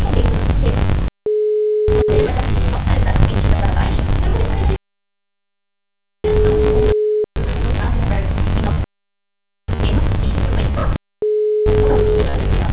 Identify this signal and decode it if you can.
Ringback tone (Europe, 425 Hz), with noise.